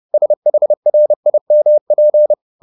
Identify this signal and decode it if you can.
Morse code, heard as a keyed tone, about 30 wpm, 600 Hz. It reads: SHRIMP
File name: the old man and the sea / the old man and the sea 1 30wpm